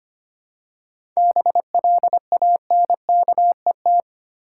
Morse code, heard as a keyed tone, about 25 wpm, 700 Hz. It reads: BLANKET
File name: practice / 25wpm / Easy Words with Hints - 25wpm